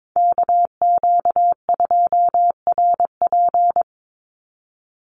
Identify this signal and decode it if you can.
Morse code, heard as a keyed tone, about 22 wpm, 700 Hz. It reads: KQ2RP
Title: NA SOTA Chaser Callsigns 22wpm